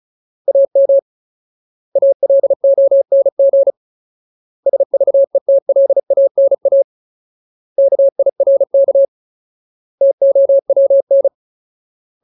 Morse code, heard as a keyed tone, about 35 wpm, 550 Hz. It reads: AM ALONG SVETLANA KIRK TOWN
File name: Ultimate Word List / 35WPM Ultimate